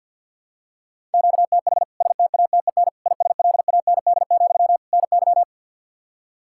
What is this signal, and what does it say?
Morse code, heard as a keyed tone, about 50 wpm, 700 Hz. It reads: QTH STATEN ISLAND, NY
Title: K1USN SST QSO Practice 50wpm